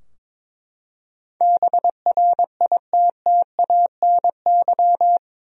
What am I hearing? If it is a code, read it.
Morse code, heard as a keyed tone, about 22 wpm, 700 Hz. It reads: BRITTANY
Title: USA Common Names - 22wpm